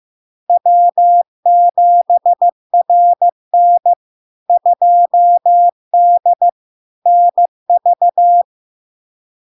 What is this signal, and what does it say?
Morse code, heard as a keyed tone, about 15 wpm, 700 Hz. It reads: W7RN 2D NV
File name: LARK-FD-2022-Log-in-CW-15wpm-700-Hz